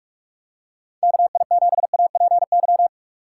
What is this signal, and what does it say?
Morse code, heard as a keyed tone, about 46 wpm, 700 Hz. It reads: KI7APY
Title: USA Call Signs - 2x Character Spacing - 45wpm